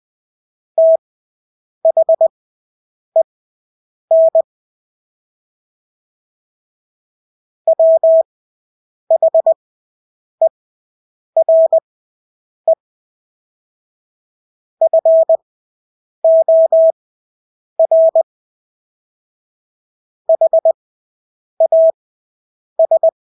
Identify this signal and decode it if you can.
Morse code, heard as a keyed tone, about 20 wpm, spaced as for 8 wpm, 650 Hz. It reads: THEN WHERE FOR HAS
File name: Engelske ord 20/8 wpm